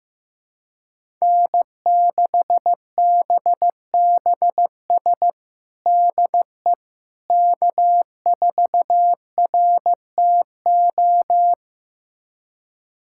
Morse code, heard as a keyed tone, about 15 wpm, 700 Hz. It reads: N6BBS DE K4RTO
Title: USA Callsign DE Callsign - 5 Characters - 15wpm